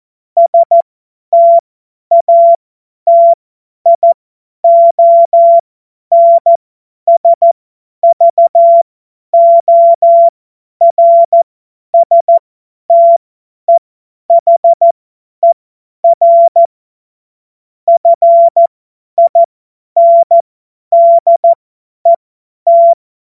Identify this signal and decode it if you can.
Morse code, heard as a keyed tone, about 14 wpm, spaced as for 10 wpm, 680 Hz. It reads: STATIONSVORSTEHER FINDET
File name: Morsekurs / Witze / Witz 02 Januar 2024 50 F70